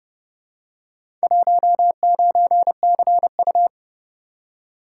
Morse code, heard as a keyed tone, about 29 wpm, 700 Hz. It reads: 19CU